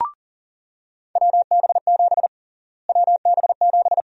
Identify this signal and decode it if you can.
Morse code, heard as a keyed tone, about 40 wpm, 700 Hz. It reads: WB7 WB7
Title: USA Call Sign Prefixes - Repeated 2x - 40wpm